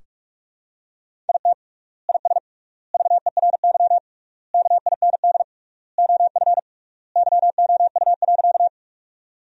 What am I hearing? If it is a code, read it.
Morse code, heard as a keyed tone, about 43 wpm, 700 Hz. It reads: IT IS VERY KIND OF YOU.
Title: 3x Word-Spacing - Sentences from Top 200 Words - 45wpm